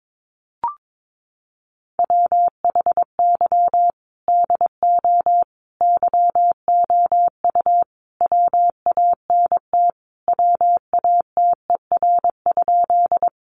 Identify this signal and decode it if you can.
Morse code, heard as a keyed tone, about 22 wpm, 700 Hz. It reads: WHY DO YOU WANT WATER?
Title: Sentences from Top 200 Words 22wpm